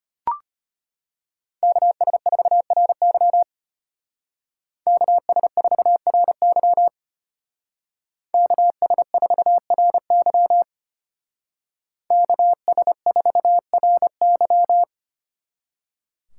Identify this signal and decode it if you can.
Morse code, heard as a keyed tone, about 30 wpm, 700 Hz. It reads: KS4RY KS4RY KS4RY KS4RY
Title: Speed Racing - USA 2x2 Call Signs 25wpm